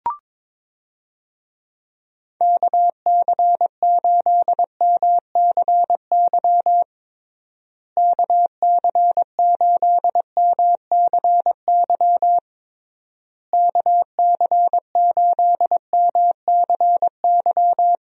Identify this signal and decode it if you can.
Morse code, heard as a keyed tone, about 22 wpm, 700 Hz. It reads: KC8MCY KC8MCY KC8MCY
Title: USA Call Signs - 6 Characters - Repeated 3x - 22wpm